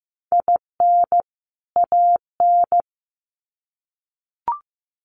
Morse code, heard as a keyed tone, about 15 wpm, 700 Hz. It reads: IN AN
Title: Sets of 2 Words 15wpm